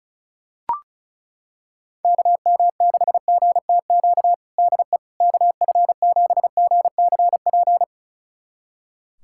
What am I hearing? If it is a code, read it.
Morse code, heard as a keyed tone, about 34 wpm, 700 Hz. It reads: KM6GTQ DE KF7GCP
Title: USA Callsign DE Callsign - 6 Characters - 35wpm